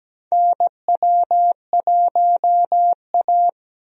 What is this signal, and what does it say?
Morse code, heard as a keyed tone, about 17 wpm, 700 Hz. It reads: NW1A
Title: USA Call Signs - 4 Characters - Repeated 3x - 17wpm